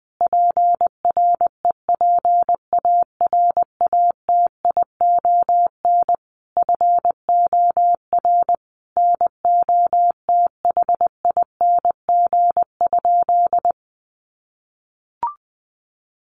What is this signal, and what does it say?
Morse code, heard as a keyed tone, about 20 wpm, 700 Hz. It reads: PREPARATION FOR NOTHING?
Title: Sentences from Top 1400 Words 20wpm